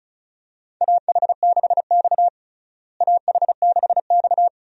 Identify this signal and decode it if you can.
Morse code, heard as a keyed tone, about 34 wpm, 700 Hz. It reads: AH6X AH6X
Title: USA Call Signs - 4 Characters - Repeated 2x - 35wpm